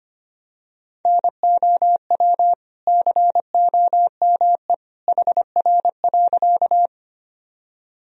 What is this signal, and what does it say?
Morse code, heard as a keyed tone, about 25 wpm, 700 Hz. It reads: NOW COME HR.
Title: Sentences from Top 200 Words and Abbreviations 25wpm